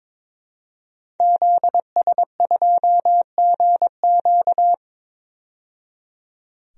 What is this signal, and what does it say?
Morse code, heard as a keyed tone, about 21 wpm, 700 Hz. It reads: ZS2GQ
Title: DXCC-100 Call Signs - 5 Characters - 22wpm